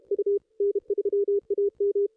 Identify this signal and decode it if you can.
Morse code, heard as a keyed tone, about 34 wpm, 440 Hz. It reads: U N3AM
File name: contests / ss-cw-k5tr-op-n5rz -1900